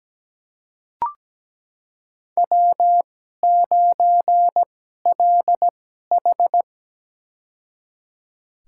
Morse code, heard as a keyed tone, about 17 wpm, spaced as for 12 wpm, 700 Hz. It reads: W9LH
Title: USA Call Signs - 4 Characters - 2x Character Spacing - 17wpm